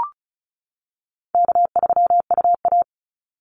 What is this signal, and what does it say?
Morse code, heard as a keyed tone, about 34 wpm, 700 Hz. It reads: K3UA
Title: Call Signs - 35wpm